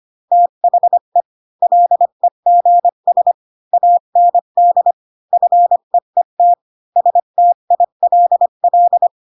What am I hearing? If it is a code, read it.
Morse code, heard as a keyed tone, about 25 wpm, 700 Hz. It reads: THE LEGS AND FEET STILL
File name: Pinocchio / Pinocchio ch3 20 25 ik1hge